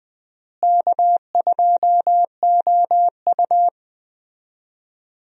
Morse code, heard as a keyed tone, about 20 wpm, 700 Hz. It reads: K2OU
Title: USA Call Signs - 4 Characters - Repeated 3x - 20wpm